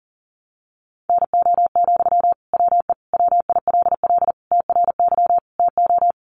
Morse code, heard as a keyed tone, about 40 wpm, 700 Hz. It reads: NO, WE WILL TRY TO
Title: Sentences from Top 1100 Words 40wpm